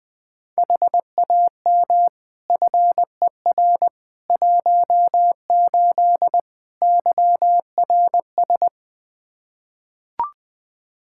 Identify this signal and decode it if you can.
Morse code, heard as a keyed tone, about 21 wpm, 700 Hz. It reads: HAM FER 18 YRS
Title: Practice QSO - Exchange 3 - More Info - 20wpm